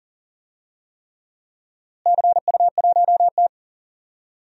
Morse code, heard as a keyed tone, about 38 wpm, 700 Hz. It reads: CU1T
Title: DXCC-100 Call Signs - 4 Characters - Repeated 3x - 40wpm